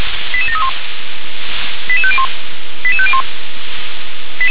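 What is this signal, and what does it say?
Five-tone selective call (ZVEI), heard as quick runs of five tones, tones of about 80 ms. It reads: 88731 88501 88501
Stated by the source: ZVEI five-tone call received by SDR.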